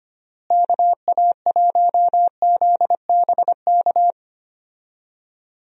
Morse code, heard as a keyed tone, about 25 wpm, 700 Hz. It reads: KA1ZBK
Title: USA Call Signs - 6 Characters - Repeated 3x - 25wpm